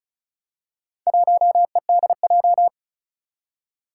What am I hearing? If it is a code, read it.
Morse code, heard as a keyed tone, about 36 wpm, 700 Hz. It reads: 1EDJ
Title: Four Challenging Letter-Number Combinations 35wpm